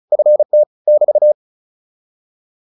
Morse code, heard as a keyed tone, about 34 wpm, 600 Hz. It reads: FT <BT>